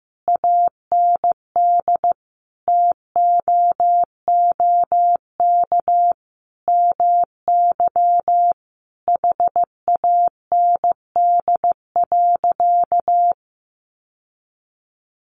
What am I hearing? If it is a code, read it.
Morse code, heard as a keyed tone, about 15 wpm, 700 Hz. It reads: AND TOOK MY HAND.
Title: Sentences from Top 1100 Words 15wpm